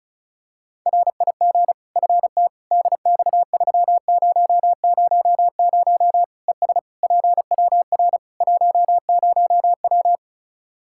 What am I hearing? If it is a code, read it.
Morse code, heard as a keyed tone, about 34 wpm, 700 Hz. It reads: RIG FT DX3000 ES PWR 10W
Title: Practice QSO - Exchange 2 - Info - 35wpm